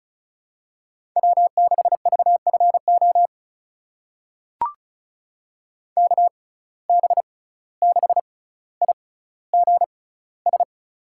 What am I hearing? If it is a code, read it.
Morse code, heard as a keyed tone, about 36 wpm, spaced as for 17 wpm, 700 Hz. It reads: W6VFO KB6IGS
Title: USA Call Signs - 6x Character Spacing - 35wpm